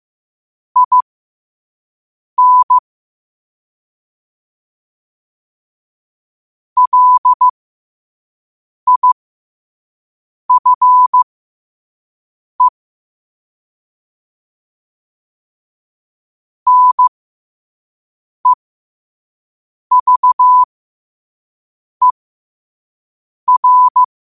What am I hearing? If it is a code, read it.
Morse code, heard as a keyed tone, about 15 wpm, spaced as for 5 wpm, 1000 Hz. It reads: IN LIFE NEVER
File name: qotdfiles / 5wpm